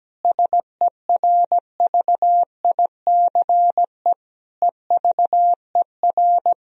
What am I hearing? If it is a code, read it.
Morse code, heard as a keyed tone, about 17 wpm, 700 Hz. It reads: SERVICE EVER
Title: Sentences from Top 1000 Words - Review Part 4 - 17wpm